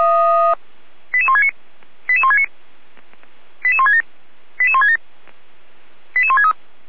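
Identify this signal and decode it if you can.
Five-tone selective call (ZVEI), heard as quick runs of five tones, tones of about 70 ms. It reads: 88169 88169 88167 88167 88163
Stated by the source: ZVEI five-tone call received on SDR.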